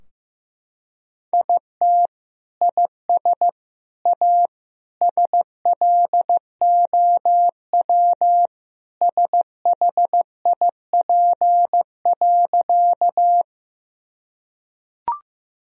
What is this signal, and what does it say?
Morse code, heard as a keyed tone, about 15 wpm, 700 Hz. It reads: IT IS A SLOW SHIP.
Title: Sentences from Top 1000 Words - Review Part 2 - 15wpm